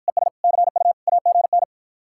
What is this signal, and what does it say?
Morse code, heard as a keyed tone, about 52 wpm, 700 Hz. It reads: ES CU AGN